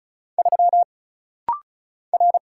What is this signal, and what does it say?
Morse code, heard as a keyed tone, about 33 wpm, 700 Hz. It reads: 3 R
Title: Single Letter-Number Alt - Rapid-Fire 35wpm